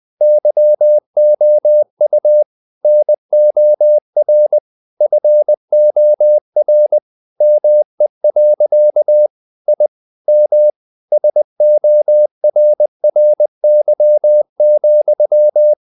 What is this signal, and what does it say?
Morse code, heard as a keyed tone, about 20 wpm, 600 Hz. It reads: YOU NOR FOR ME. I M SORRY,
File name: the old man and the sea part 2 20wpm